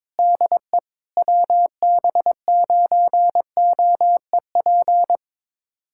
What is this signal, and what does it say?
Morse code, heard as a keyed tone, about 22 wpm, 700 Hz. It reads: DE WB9OEP
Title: USA Callsign DE Callsign - 6 Characters - 22wpm